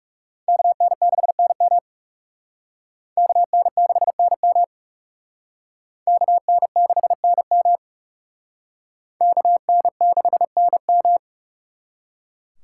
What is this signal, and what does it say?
Morse code, heard as a keyed tone, about 39 wpm, 700 Hz. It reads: KN6NM KN6NM KN6NM KN6NM